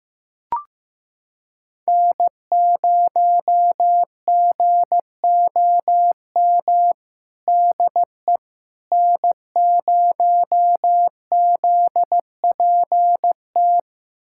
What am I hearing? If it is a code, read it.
Morse code, heard as a keyed tone, about 16 wpm, 700 Hz. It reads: N0GOM DE N0ZPT
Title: USA Callsign DE Callsign - 5 Characters - 15wpm